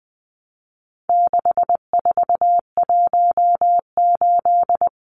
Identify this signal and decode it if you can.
Morse code, heard as a keyed tone, about 20 wpm, 700 Hz. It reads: 6418
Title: Numbers - 4-Digit 20wpm